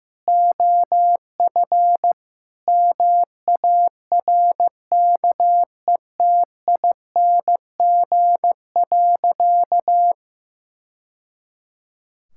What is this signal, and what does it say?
Morse code, heard as a keyed tone, about 15 wpm, 700 Hz. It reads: OF MARKETING.
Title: Sentences from Top 1000 Words - Review Part 5 - 15wpm